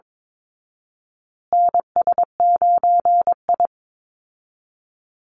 Morse code, heard as a keyed tone, about 22 wpm, 700 Hz. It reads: NS9I
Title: Call Signs Repeated 2x - 22wpm